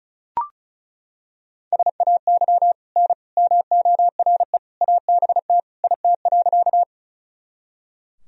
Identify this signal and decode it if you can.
Morse code, heard as a keyed tone, about 34 wpm, 700 Hz. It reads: SAY N MORE ABT IT.
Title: Sentences from Top 200 Words and Abbreviations 35wpm